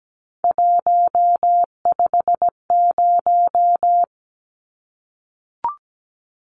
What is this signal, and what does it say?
Morse code, heard as a keyed tone, about 17 wpm, 700 Hz. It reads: 150
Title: Numbers - 3-Digit 17wpm